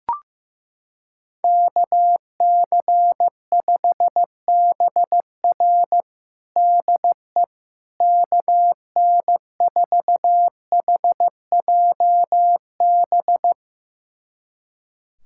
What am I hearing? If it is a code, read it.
Morse code, heard as a keyed tone, about 15 wpm, 700 Hz. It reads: KC5BR DE KN4HJB
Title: USA Callsign DE Callsign 15wpm